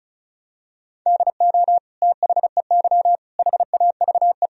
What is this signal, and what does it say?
Morse code, heard as a keyed tone, about 34 wpm, 700 Hz. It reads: DO THEY HAVE
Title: Sentences from Top 1000 Words - Review Part 4 - 35wpm